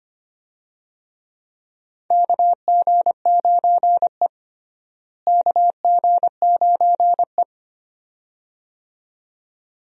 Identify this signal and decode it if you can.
Morse code, heard as a keyed tone, about 25 wpm, 700 Hz. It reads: KG9E KG9E
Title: USA Call Signs - 4 Characters - Repeated 2x - 25wpm